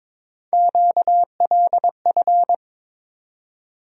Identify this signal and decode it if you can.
Morse code, heard as a keyed tone, about 22 wpm, 700 Hz. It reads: QLF
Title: USA Call Sign Postfixes - 3 Letters 22wpm